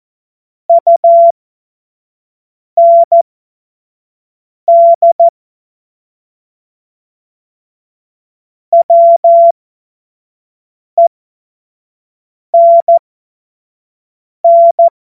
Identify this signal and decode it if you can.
Morse code, heard as a keyed tone, about 14 wpm, spaced as for 5 wpm, 680 Hz. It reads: UND WENN
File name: Morsekurs / Witze / Witz 20. Juli 2021, 25 F70